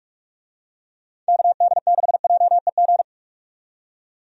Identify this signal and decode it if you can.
Morse code, heard as a keyed tone, about 47 wpm, 700 Hz. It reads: KD6JEG
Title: USA Call Signs - 6 Characters - 45wpm